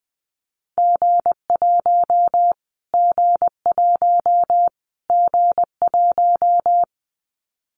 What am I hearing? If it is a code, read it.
Morse code, heard as a keyed tone, about 20 wpm, 700 Hz. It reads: G1 G1 G1
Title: Callsign Prefixes for Top 25 DXCC - Repeated 3x - 20wpm